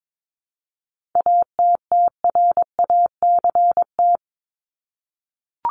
Morse code, heard as a keyed tone, about 22 wpm, 700 Hz. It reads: ATTRACT
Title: Top 1501-1750 Words 22wpm